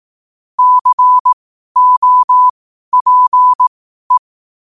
Morse code, heard as a keyed tone, about 18 wpm, spaced as for 12 wpm, 1000 Hz. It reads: COPE
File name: qotdfiles / 12wpm